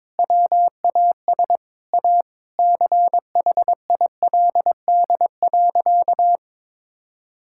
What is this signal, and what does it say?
Morse code, heard as a keyed tone, about 22 wpm, 700 Hz. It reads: WAS A CHILD.
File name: practice / 22wpm / Sentences with 4-Word Phrases - 22wpm